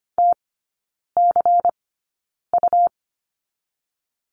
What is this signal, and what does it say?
Morse code, heard as a keyed tone, about 25 wpm, spaced as for 9 wpm, 700 Hz. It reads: TCU
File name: Lesson 17e - Letter u - Three Characters 25wpm